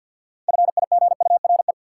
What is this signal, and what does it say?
Morse code, heard as a keyed tone, about 53 wpm, 700 Hz. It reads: FIGURE
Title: Sentences from Top 1000 Words - Review Part 2 - New Word Drill - 50wpm